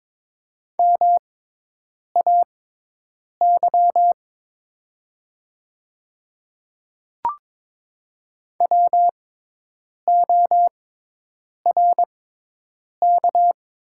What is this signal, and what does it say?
Morse code, heard as a keyed tone, about 22 wpm, spaced as for 8 wpm, 700 Hz. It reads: MAY WORK
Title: 6x Character Spacing - Top 200 Words - 22wpm